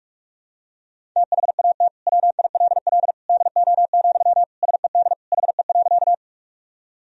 Morse code, heard as a keyed tone, about 46 wpm, 700 Hz. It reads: THAT WILL DO, SED HE.
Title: Sentences from Top 100 Words and Abbreviations 45wpm